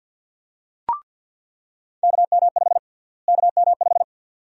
Morse code, heard as a keyed tone, about 47 wpm, 700 Hz. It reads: KM5 KM5
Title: USA Call Sign Prefixes - Repeated 2x - 50wpm